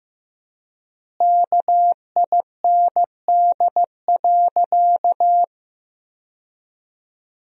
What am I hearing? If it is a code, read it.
Morse code, heard as a keyed tone, about 15 wpm, 700 Hz. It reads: KIND.
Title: 4x Word-Spacing - Sentences from Top 200 Words - 15wpm